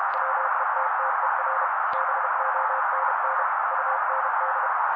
Morse code, heard as a keyed tone, about 32 wpm, 530 Hz. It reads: WETTER BONN UND
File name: daily / 31wpm noise